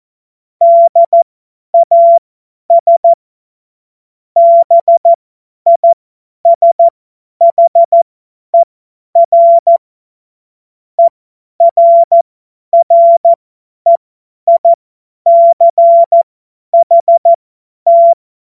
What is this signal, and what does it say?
Morse code, heard as a keyed tone, about 14 wpm, spaced as for 10 wpm, 680 Hz. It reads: DAS BISHER ERREICHT
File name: Morsekurs / Witze / Witz 05 März 2024 50 F70